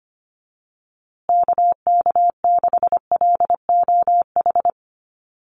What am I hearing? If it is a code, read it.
Morse code, heard as a keyed tone, about 25 wpm, 700 Hz. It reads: KK6LOH
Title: USA Call Signs Repeated 2x 25wpm